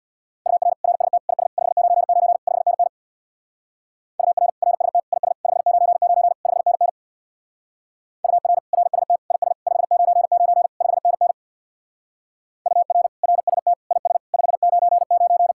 Morse code, heard as a keyed tone, about 66 wpm, 700 Hz. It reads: UR RST IS 599 5NN UR RST IS 599 5NN UR RST IS 599 5NN UR RST IS 599